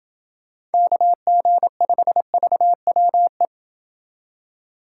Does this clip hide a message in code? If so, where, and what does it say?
Yes: Morse code from 0.7 to 3.5 s, heard as a keyed tone, about 28 wpm, 700 Hz. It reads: KG5VWE